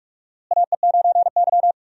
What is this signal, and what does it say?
Morse code, heard as a keyed tone, about 46 wpm, 700 Hz. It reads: AE9Y
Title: USA Call Signs - 4 Characters - Repeated 2x - 45wpm